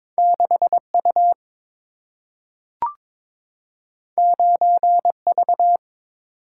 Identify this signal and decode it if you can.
Morse code, heard as a keyed tone, about 22 wpm, 700 Hz. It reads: 6U 9V
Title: Single-Letter Single-Number 22wpm